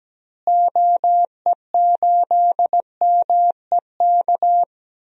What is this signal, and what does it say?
Morse code, heard as a keyed tone, about 17 wpm, 700 Hz. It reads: OE8MEK
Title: DXCC-100 Call Signs - 6 Characters - Repeated 2x - 17wpm